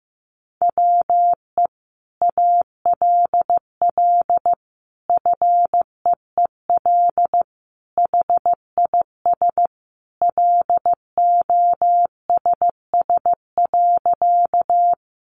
Morse code, heard as a keyed tone, about 15 wpm, 700 Hz. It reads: WE ALL FEEL HIS LOSS.
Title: Sentences from Top 1400 Words 15wpm